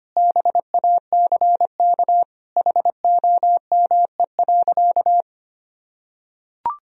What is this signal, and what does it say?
Morse code, heard as a keyed tone, about 25 wpm, 700 Hz. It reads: BACK HOME.
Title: Sentences from Top 1000 Words - Review Part 4 - 25wpm